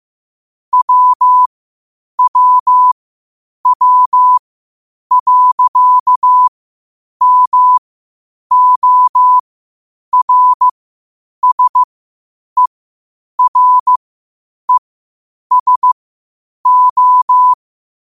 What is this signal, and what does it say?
Morse code, heard as a keyed tone, about 15 wpm, spaced as for 8 wpm, 1000 Hz. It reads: WWW.MORSERESO
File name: qotdfiles / 8wpm